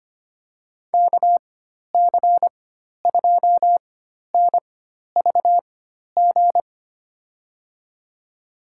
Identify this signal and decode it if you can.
Morse code, heard as a keyed tone, about 25 wpm, spaced as for 12 wpm, 700 Hz. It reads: KC2NVG
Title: USA Call Signs - 6 Characters - 4x Character Spacing - 25wpm